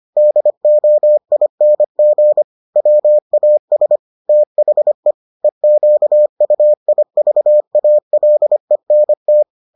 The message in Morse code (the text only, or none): DOING WAS THE EQUIVALENT